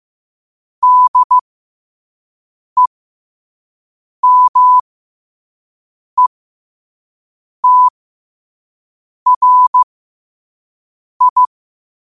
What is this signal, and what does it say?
Morse code, heard as a keyed tone, about 15 wpm, spaced as for 5 wpm, 1000 Hz. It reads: DEMETRI
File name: qotdfiles / 5wpm